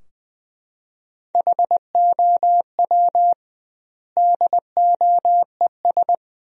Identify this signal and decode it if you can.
Morse code, heard as a keyed tone, about 20 wpm, 700 Hz. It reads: HOW DOES
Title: 2x Word-Spacing - Sentences from Top 200 Words - 20wpm